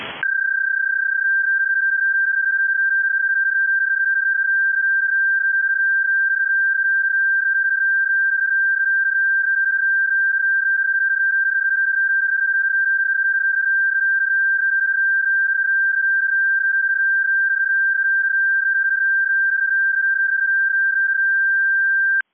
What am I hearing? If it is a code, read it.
PSK31, about 31.25 baud.